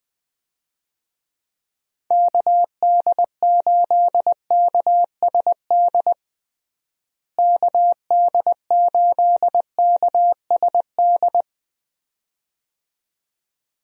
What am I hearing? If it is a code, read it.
Morse code, heard as a keyed tone, about 21 wpm, 700 Hz. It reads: KD8KSD KD8KSD
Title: USA Call Signs - 6 Characters - Repeated 2x - 20wpm